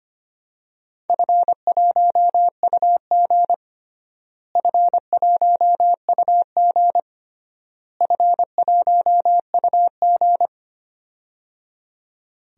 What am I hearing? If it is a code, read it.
Morse code, heard as a keyed tone, about 25 wpm, 700 Hz. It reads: F1UG F1UG F1UG
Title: DXCC-100 Call Signs - 4 Characters - Repeated 3x - 25wpm